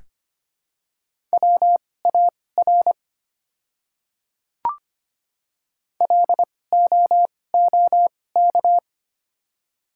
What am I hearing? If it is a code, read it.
Morse code, heard as a keyed tone, about 25 wpm, spaced as for 18 wpm, 700 Hz. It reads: WAR LOOK
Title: 2x Character Spacing - Top 500 Words - 25wpm